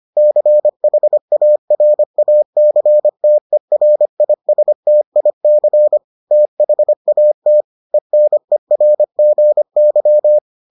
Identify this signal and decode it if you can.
Morse code, heard as a keyed tone, about 25 wpm, 600 Hz. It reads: CHARACTERISTIC THAT ENERGY